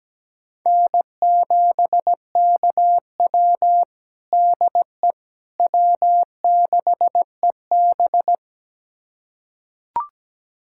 Morse code, heard as a keyed tone, about 17 wpm, 700 Hz. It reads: N7KW DE W6EB
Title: USA Callsign DE Callsign - 4 Characters - 17wpm